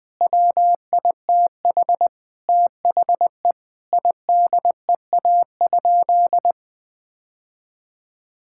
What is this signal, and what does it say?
Morse code, heard as a keyed tone, about 20 wpm, 700 Hz. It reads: WITH THE IDEA?
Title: Sentences with 8-Word Phrases - Spoken Beginning - 20wpm